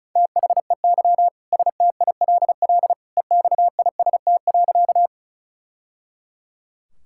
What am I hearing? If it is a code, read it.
Morse code, heard as a keyed tone, about 34 wpm, 700 Hz. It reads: THEY STILL EXIST.
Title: Sentences from Top 1000 Words - Review Part 5 - 35wpm